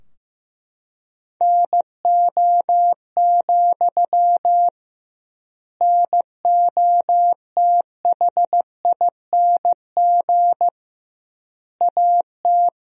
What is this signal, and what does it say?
Morse code, heard as a keyed tone, about 15 wpm, 700 Hz. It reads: NO, NOTHING AT
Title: 2x Word-Spacing - Sentences from Top 500 Words - 15wpm